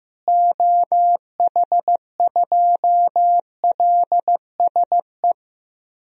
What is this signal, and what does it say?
Morse code, heard as a keyed tone, about 15 wpm, 700 Hz. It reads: OH2LSE
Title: DXCC-100 Call Signs - 6 Characters - Repeated 3x - 15wpm